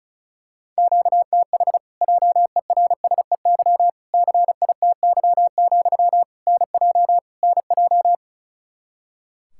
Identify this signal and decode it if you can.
Morse code, heard as a keyed tone, about 34 wpm, 700 Hz. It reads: QTH JERSEY CITY, NJ NJ
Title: Practice QSO - Exchange 1 - Beginning - 35wpm